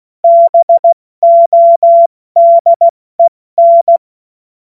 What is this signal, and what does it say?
Morse code, heard as a keyed tone, about 16 wpm, spaced as for 14 wpm, 680 Hz. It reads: BODEN
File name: Morsekurs / Witze / Witz 02 April 2024 70 F80